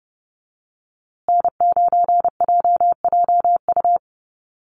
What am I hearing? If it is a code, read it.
Morse code, heard as a keyed tone, about 30 wpm, 700 Hz. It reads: N9JJU
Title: USA Call Signs Repeated 3x 30wpm